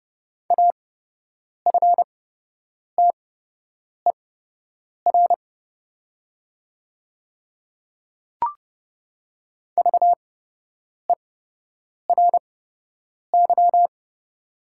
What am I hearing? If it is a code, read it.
Morse code, heard as a keyed tone, about 30 wpm, spaced as for 8 wpm, 700 Hz. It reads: AFTER VERY
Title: 8x Character Spacing - Top 300 Words - 30wpm